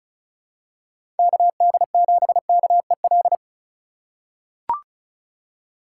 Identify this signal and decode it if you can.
Morse code, heard as a keyed tone, about 36 wpm, 700 Hz. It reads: KD7KEL
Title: USA Call Signs - 4x Character Spacing - 35wpm